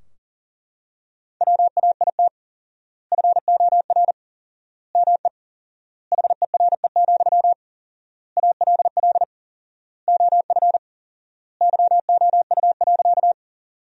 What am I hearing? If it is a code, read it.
Morse code, heard as a keyed tone, about 37 wpm, 700 Hz. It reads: WAIT FOR ME HERE, ALL OF YOU.
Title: 4x Word-Spacing - Sentences from Top 500 Words - 40wpm